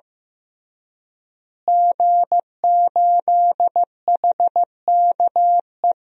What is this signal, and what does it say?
Morse code, heard as a keyed tone, about 15 wpm, 700 Hz. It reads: G8HKE
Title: DXCC-100 Call Signs - 5 Characters - Repeated 3x - 15wpm